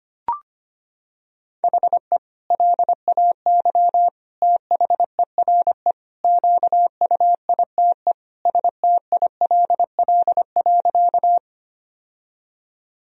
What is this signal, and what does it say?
Morse code, heard as a keyed tone, about 25 wpm, 700 Hz. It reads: HE LAY THERE QUITE STILL.
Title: Sentences from Top 1000 Words - Review Part 2 - 25wpm